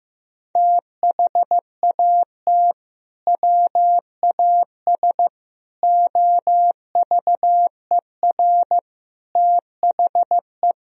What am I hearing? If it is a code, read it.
Morse code, heard as a keyed tone, about 15 wpm, 700 Hz. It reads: THAT WAS OVER THE
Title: Sentences from Top 1000 Words - Review Part 2 - 15wpm